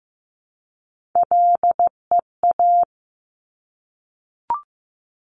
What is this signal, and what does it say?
Morse code, heard as a keyed tone, about 15 wpm, 700 Hz. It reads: LEA
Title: Top Three Letter Combinations in Words 15wpm